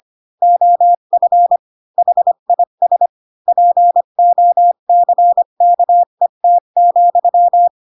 Morse code, heard as a keyed tone, about 25 wpm, 700 Hz. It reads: OF HIS POCKET,
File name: Pinocchio / Pinocchio ch7 20 25 ik1hge